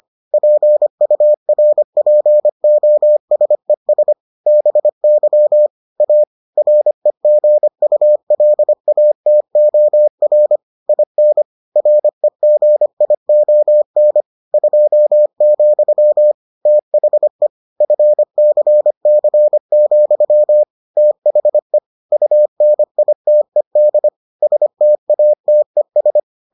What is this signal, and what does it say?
Morse code, heard as a keyed tone, about 25 wpm, 600 Hz. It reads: PURPOSES BY A REGULATOR IN REGION 2, THE FCC, THE UNITED STATES